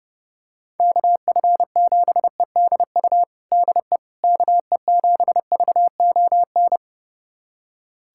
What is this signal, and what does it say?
Morse code, heard as a keyed tone, about 30 wpm, 700 Hz. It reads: KF7EDU DE KE7VON
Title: USA Callsign DE Callsign - 6 Characters - 30wpm